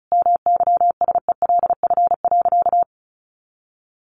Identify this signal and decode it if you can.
Morse code, heard as a keyed tone, about 34 wpm, 700 Hz. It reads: MYSELF.
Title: Sentences from Top 1300 Words 35wpm